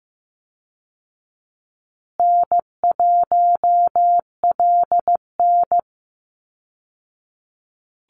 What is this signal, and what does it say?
Morse code, heard as a keyed tone, about 15 wpm, 700 Hz. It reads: N1LN
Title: Call Signs Repeated 3x - 15wpm